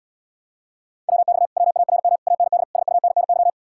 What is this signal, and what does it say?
Morse code, heard as a keyed tone, about 80 wpm, 700 Hz. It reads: WX WARM AND SUNNY